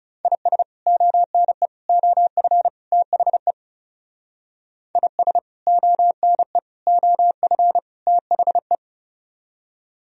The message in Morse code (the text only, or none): IS ONE OF THE IS ONE OF THE